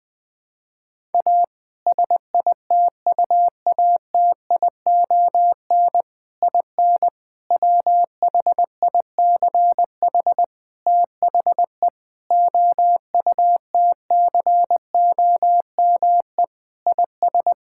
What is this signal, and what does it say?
Morse code, heard as a keyed tone, about 20 wpm, 700 Hz. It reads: A SITUATION IN WHICH THE OUTCOME IS
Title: Sentences with 3-Word Phrases - Spoken Beginning - 20wpm